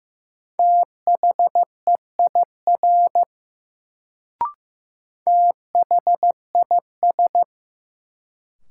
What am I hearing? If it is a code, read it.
Morse code, heard as a keyed tone, about 15 wpm, 700 Hz. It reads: THEIR THIS
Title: Sets of 2 Words - New Word Drill - 15wpm